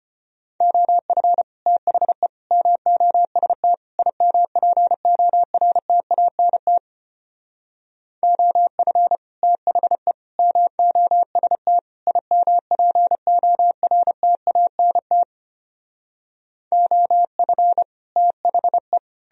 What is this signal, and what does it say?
Morse code, heard as a keyed tone, about 33 wpm, 700 Hz. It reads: OF THE MOST IMPORTANT OF THE MOST IMPORTANT OF THE